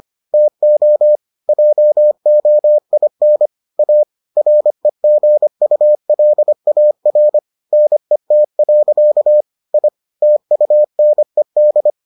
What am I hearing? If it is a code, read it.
Morse code, heard as a keyed tone, about 25 wpm, 600 Hz. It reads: TO JOIN A REGULAR NET. I TUNED